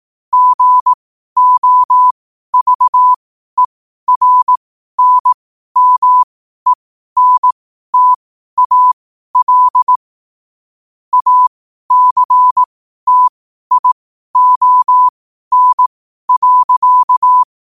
Morse code, heard as a keyed tone, about 18 wpm, spaced as for 12 wpm, 1000 Hz. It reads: GOVERNMENTAL ACTION.
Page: New quotes every day in morse code at 12 Words per minute.